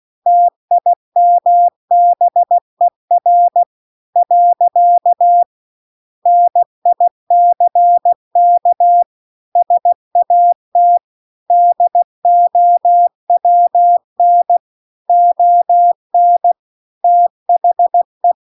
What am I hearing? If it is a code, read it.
Morse code, heard as a keyed tone, about 16 wpm, 700 Hz. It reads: TIMBER . NICK SAT DOWN ON THE
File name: CW-The-Big-Twoheaded-River-1-16WPM